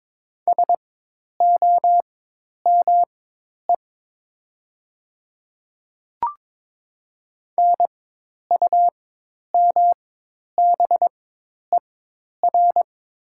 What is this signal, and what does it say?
Morse code, heard as a keyed tone, about 22 wpm, spaced as for 10 wpm, 700 Hz. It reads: SOME NUMBER
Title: 4x Character Spacing - Top 100 Words - 22wpm